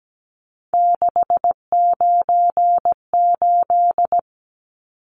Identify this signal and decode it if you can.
Morse code, heard as a keyed tone, about 17 wpm, 700 Hz. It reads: 698
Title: Numbers - 3-Digit 17wpm